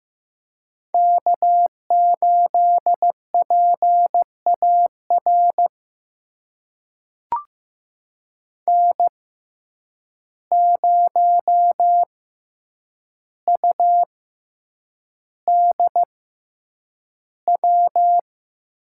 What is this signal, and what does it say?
Morse code, heard as a keyed tone, about 15 wpm, spaced as for 8 wpm, 700 Hz. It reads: K8PAR N0UDW
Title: USA Call Signs - 5 Characters - 6x Character Spacing - 15wpm